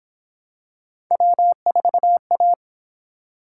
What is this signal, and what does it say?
Morse code, heard as a keyed tone, about 27 wpm, 700 Hz. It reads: W4A
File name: Speed-Racing - Three Letter-Number Combinations - 22wpm